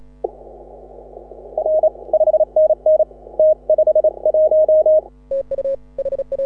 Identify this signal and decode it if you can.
Morse code, heard as a keyed tone, about 28 wpm, 620 Hz. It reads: R 5NN T51
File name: AUDIO2018 / HA6OI CW